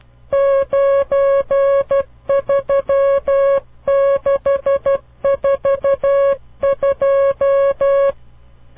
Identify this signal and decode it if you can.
Morse code, heard as a keyed tone, about 12 wpm, 550 Hz. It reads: 93642
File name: CW Learning Programs / farnsworth / unit 5